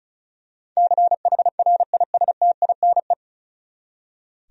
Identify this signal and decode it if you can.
Morse code, heard as a keyed tone, about 34 wpm, 700 Hz. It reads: CHRISTINE
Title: USA Common Names - 35wpm